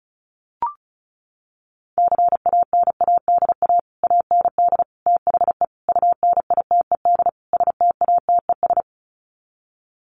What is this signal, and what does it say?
Morse code, heard as a keyed tone, about 34 wpm, 700 Hz. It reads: CANADA AND THE UNITED STATES
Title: Sets of 5 Words Encore 35wpm